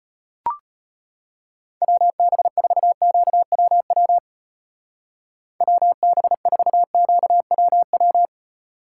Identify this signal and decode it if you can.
Morse code, heard as a keyed tone, about 32 wpm, 700 Hz. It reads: WB4QWW WB4QWW